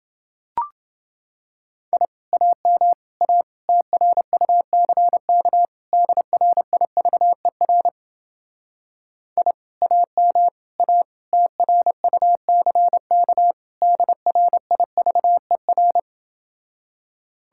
Morse code, heard as a keyed tone, about 30 wpm, 700 Hz. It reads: I AM A TRUCK DRIVER I AM A TRUCK DRIVER